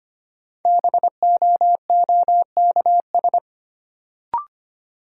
Morse code, heard as a keyed tone, about 25 wpm, 700 Hz. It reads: BOOKS
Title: Sentences from Top 1000 Words - Review Part 4 - New Word Drill - 25wpm